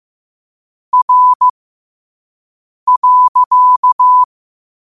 Morse code, heard as a keyed tone, about 15 wpm, spaced as for 5 wpm, 1000 Hz. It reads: R.
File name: qotdfiles / 5wpm